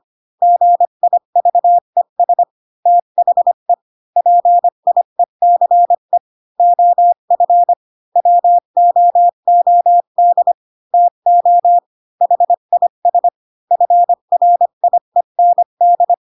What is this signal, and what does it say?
Morse code, heard as a keyed tone, about 25 wpm, 700 Hz. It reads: GIVES THE PIECE OF WOOD TO HIS FRIEND
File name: Pinocchio / Pinocchio ch2 20 25 ik1hge